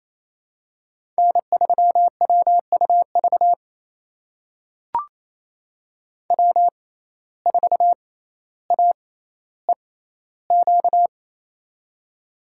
Morse code, heard as a keyed tone, about 28 wpm, spaced as for 14 wpm, 700 Hz. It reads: N3WUV W4AEQ
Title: USA Call Signs - 5 Characters - 6x Character Spacing - 28wpm